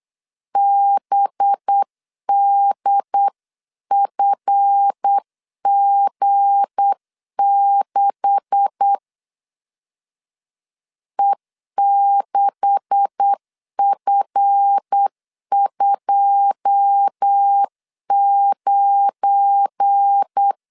Morse code, heard as a keyed tone, about 8 wpm, 800 Hz. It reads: BDFG6 E6F29
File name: CW / DET / 1 9 1500C S A 8wpm 5637